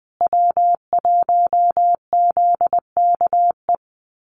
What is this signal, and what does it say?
Morse code, heard as a keyed tone, about 20 wpm, 700 Hz. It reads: W1ZKE
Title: USA Call Signs Repeated 3x 20wpm